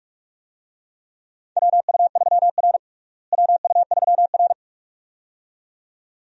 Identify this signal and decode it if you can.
Morse code, heard as a keyed tone, about 44 wpm, 700 Hz. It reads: WU3R WU3R
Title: USA Call Signs - 4 Characters - Repeated 2x - 45wpm